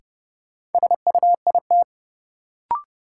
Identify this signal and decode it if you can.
Morse code, heard as a keyed tone, about 31 wpm, 700 Hz. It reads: SUIT
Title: Sentences from Top 1600 Words - New Word Drill - 30wpm